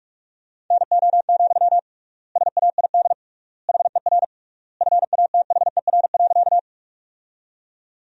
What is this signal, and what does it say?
Morse code, heard as a keyed tone, about 41 wpm, 700 Hz. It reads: NO, SAID HER FATHER.
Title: 3x Word-Spacing - Sentences from Top 300 Words - 45wpm